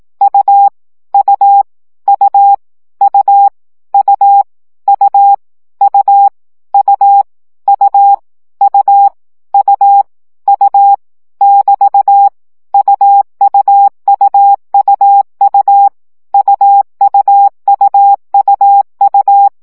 Morse code, heard as a keyed tone, about 18 wpm, 800 Hz. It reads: U U U U U U U U U U U U <BT> UUUUU UUUUU
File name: CW gyakorlas u 15 baud